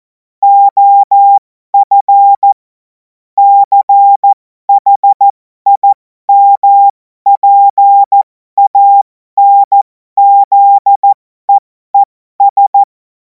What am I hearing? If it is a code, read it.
Morse code, heard as a keyed tone, about 14 wpm, spaced as for 12 wpm, 800 Hz. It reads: OF CHIMPANZEES
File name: Morsekurs / Witze